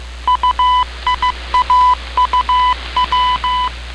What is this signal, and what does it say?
Morse code, heard as a keyed tone, about 15 wpm, 1000 Hz. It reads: UIAUW